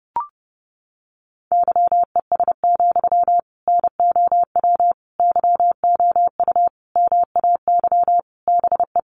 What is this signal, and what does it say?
Morse code, heard as a keyed tone, about 30 wpm, 700 Hz. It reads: YES, NOW YOU MAY BE
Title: Sentences from Top 800 Words 30wpm